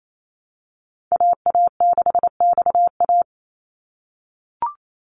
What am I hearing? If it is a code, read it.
Morse code, heard as a keyed tone, about 28 wpm, 700 Hz. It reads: AA6XA
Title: NA SOTA Activator Callsigns 28wpm